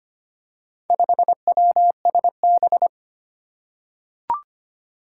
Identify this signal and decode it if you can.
Morse code, heard as a keyed tone, about 25 wpm, 700 Hz. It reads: 5WSB
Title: Four Challenging Letter-Number Combinations 25wpm